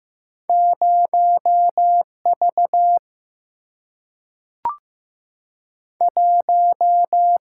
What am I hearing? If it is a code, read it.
Morse code, heard as a keyed tone, about 15 wpm, 700 Hz. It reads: 0V 1
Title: Single-Letter Single-Number 15wpm